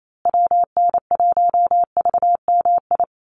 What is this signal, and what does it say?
Morse code, heard as a keyed tone, about 28 wpm, 700 Hz. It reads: WN1VMI
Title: USA Call Signs Repeated 3x 28wpm